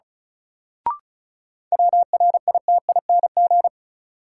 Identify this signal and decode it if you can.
Morse code, heard as a keyed tone, about 36 wpm, 700 Hz. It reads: WRITING
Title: Sentences from Top 800 Words - New Word Drill - 35wpm